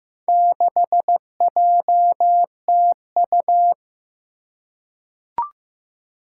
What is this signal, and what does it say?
Morse code, heard as a keyed tone, about 15 wpm, 700 Hz. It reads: 6JTU
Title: Four Challenging Letter-Number Combinations 15wpm